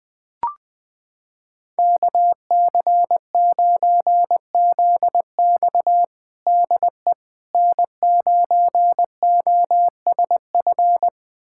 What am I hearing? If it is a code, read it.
Morse code, heard as a keyed tone, about 20 wpm, 700 Hz. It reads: KC9ZX DE N9OSF
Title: USA Callsign DE Callsign - 5 Characters - 20wpm